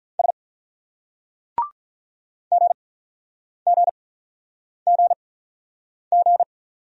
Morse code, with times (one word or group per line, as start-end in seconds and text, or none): 0.2-0.3 S
2.5-6.4 GGGG